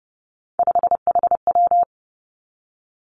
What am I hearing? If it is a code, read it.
Morse code, heard as a keyed tone, about 30 wpm, 700 Hz. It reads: 5HW
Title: Three Challenging Letter-Number Combinations 30wpm